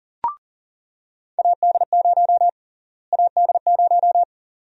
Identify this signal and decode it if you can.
Morse code, heard as a keyed tone, about 39 wpm, 700 Hz. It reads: AD0 AD0
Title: USA Call Sign Prefixes - Repeated 2x - 40wpm